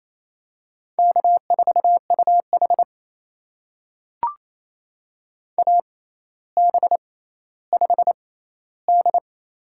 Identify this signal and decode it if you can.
Morse code, heard as a keyed tone, about 27 wpm, spaced as for 14 wpm, 700 Hz. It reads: K4UH AB5D
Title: USA Call Signs - 4 Characters - 6x Character Spacing - 28wpm